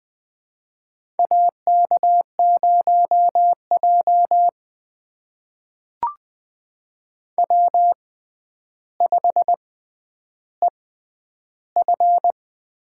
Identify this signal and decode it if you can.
Morse code, heard as a keyed tone, about 20 wpm, spaced as for 10 wpm, 700 Hz. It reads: AK0J W5EF
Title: USA Call Signs - 4 Characters - 6x Character Spacing - 20wpm